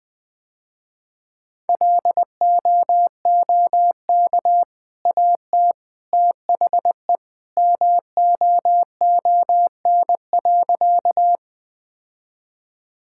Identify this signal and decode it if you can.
Morse code, heard as a keyed tone, about 20 wpm, 700 Hz. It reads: LOOK AT THE MOON.
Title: Easy Sentences with Hints - 20wpm